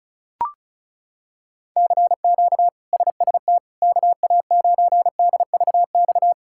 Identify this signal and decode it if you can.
Morse code, heard as a keyed tone, about 34 wpm, 700 Hz. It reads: CQ SST KA9DVX
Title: K1USN SST QSO Practice - 35wpm